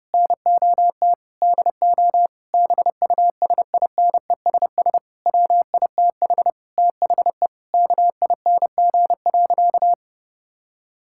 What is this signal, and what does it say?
Morse code, heard as a keyed tone, about 30 wpm, 700 Hz. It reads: NOT DO BUSINESS WITH THE KING.
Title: Sentences from Top 1000 Words - Review Part 6 - 30wpm